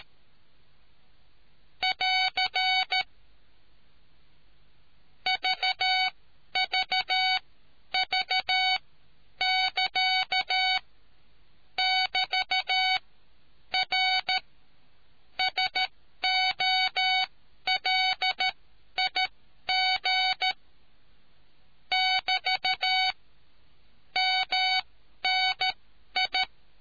Morse code, heard as a keyed tone, about 13 wpm, spaced as for 10 wpm, 750 Hz. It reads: <AR> VVV<KA> <BT>R SOLIG <BT> MNI